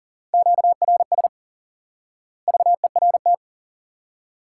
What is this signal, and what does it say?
Morse code, heard as a keyed tone, about 40 wpm, 700 Hz. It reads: QRS VERT
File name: IQE Course 3 - Lesson 89 - YAGI - Two Elements - 40wpm